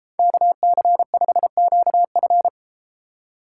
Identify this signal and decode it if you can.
Morse code, heard as a keyed tone, about 32 wpm, 700 Hz. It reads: KC5QF